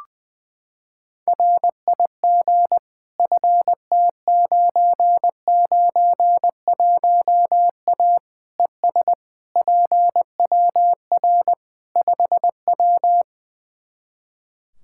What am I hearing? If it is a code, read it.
Morse code, heard as a keyed tone, about 20 wpm, 700 Hz. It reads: RIG FT991A ES PWR 5W
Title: Practice QSO - Exchange 2 - Info - 20wpm